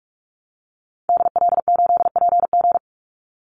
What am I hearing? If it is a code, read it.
Morse code, heard as a keyed tone, about 44 wpm, 700 Hz. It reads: DL8PG
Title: Call Signs - 45wpm